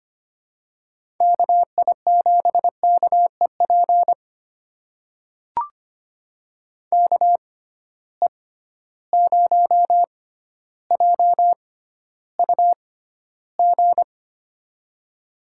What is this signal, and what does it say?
Morse code, heard as a keyed tone, about 25 wpm, spaced as for 13 wpm, 700 Hz. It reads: KI7KEP KE0JUG
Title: USA Call Signs - 6 Characters - 6x Character Spacing - 25wpm